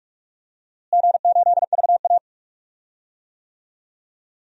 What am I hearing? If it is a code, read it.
Morse code, heard as a keyed tone, about 43 wpm, 700 Hz. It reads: G8VA
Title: DXCC-100 Call Signs - 4 Characters - Repeated 2x - 45wpm